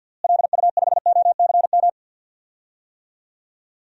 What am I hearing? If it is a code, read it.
Morse code, heard as a keyed tone, about 52 wpm, 700 Hz. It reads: LU5OCM
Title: DXCC-100 Call Signs - 6 Characters - Repeated 2x - 50wpm